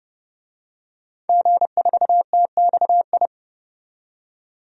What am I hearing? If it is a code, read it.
Morse code, heard as a keyed tone, about 31 wpm, 700 Hz. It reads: G4TXI
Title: DXCC-100 Call Signs - 5 Characters - 30wpm